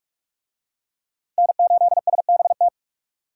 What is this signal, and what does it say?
Morse code, heard as a keyed tone, about 39 wpm, 700 Hz. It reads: N8SBT